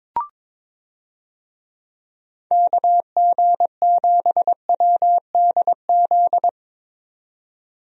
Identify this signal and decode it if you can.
Morse code, heard as a keyed tone, about 22 wpm, 700 Hz. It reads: KG7WDZ
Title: USA Call Signs - 6 Characters - 22wpm